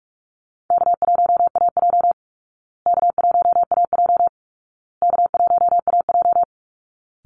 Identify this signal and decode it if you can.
Morse code, heard as a keyed tone, about 44 wpm, 700 Hz. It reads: K1AJ K1AJ K1AJ
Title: Easy Call Signs Repeated 3x - 45wpm